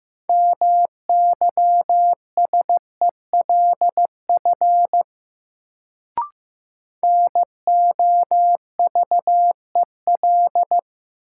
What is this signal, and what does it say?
Morse code, heard as a keyed tone, about 15 wpm, 700 Hz. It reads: MYSELF NOVEL
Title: Sentences from Top 1300 Words - New Word Drill Alt - 15wpm